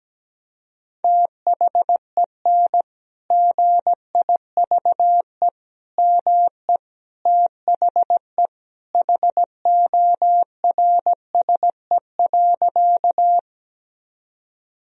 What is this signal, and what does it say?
Morse code, heard as a keyed tone, about 17 wpm, 700 Hz. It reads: THEN GIVE ME THE HORSE.
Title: Sentences from Top 1000 Words - Review Part 1 - 17wpm